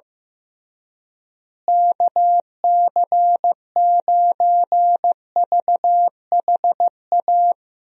Morse code, heard as a keyed tone, about 15 wpm, 700 Hz. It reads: KC9VHA
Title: USA Call Signs - 6 Characters - Repeated 3x - 15wpm